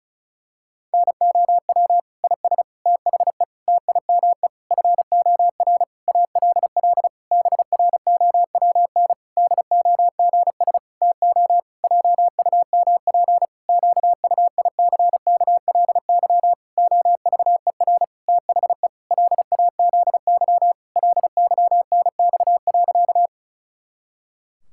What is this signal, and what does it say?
Morse code, heard as a keyed tone, about 36 wpm, 700 Hz. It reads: NOW IS THE TIME FOR ALL BROWN DOGS TO JUMP QUICKLY OVER THE LAZY LYNX.
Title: Pangram Challenge v2 - 35wpm